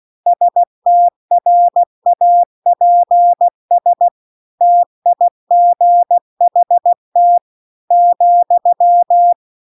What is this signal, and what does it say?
Morse code, heard as a keyed tone, about 16 wpm, 700 Hz. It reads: STRAPS TIGHT ,